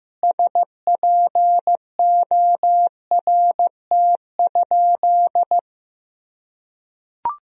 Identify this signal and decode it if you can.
Morse code, heard as a keyed tone, about 15 wpm, 700 Hz. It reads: SPORT?
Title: Sentences from Top 1000 Words - Review Part 4 - 15wpm